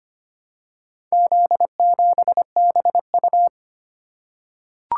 Morse code, heard as a keyed tone, about 25 wpm, 700 Hz. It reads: Z7BU
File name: practice / 25wpm / Four Letter-Number Combinations - 25wpm